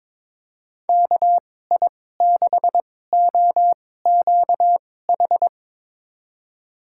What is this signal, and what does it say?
Morse code, heard as a keyed tone, about 22 wpm, spaced as for 16 wpm, 700 Hz. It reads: KI6OQH
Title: USA Call Signs - 2x Character Spacing - 22wpm